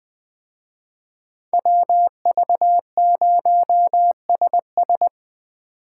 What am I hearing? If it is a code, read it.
Morse code, heard as a keyed tone, about 21 wpm, 700 Hz. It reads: WV0SS